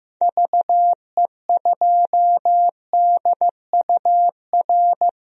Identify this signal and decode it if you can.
Morse code, heard as a keyed tone, about 15 wpm, 700 Hz. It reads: VE2DUR
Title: DXCC-100 Call Signs - 6 Characters - 15wpm